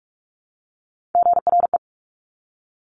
Morse code, heard as a keyed tone, about 46 wpm, 700 Hz. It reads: GRE
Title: Top Three Letter Combinations in Words 45wpm